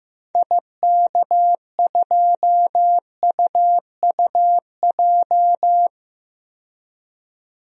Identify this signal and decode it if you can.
Morse code, heard as a keyed tone, about 15 wpm, 700 Hz. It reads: IK2UUJ
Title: DXCC-100 Call Signs - 6 Characters - 15wpm